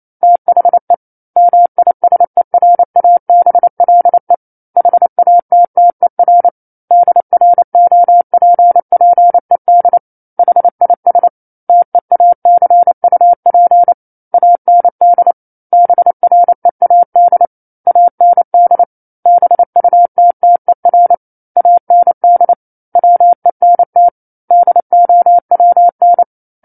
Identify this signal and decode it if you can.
Morse code, heard as a keyed tone, about 28 wpm, 700 Hz. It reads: THE MISERABLE HATTER DROPPED HIS TEACUP AND BREAD AND BUTTER AND WENT DOWN
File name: 09-Chapter-09 28 WPM